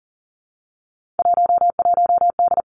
Morse code, heard as a keyed tone, about 40 wpm, 700 Hz. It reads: 11D
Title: Three Challenging Letter-Number Combinations 40wpm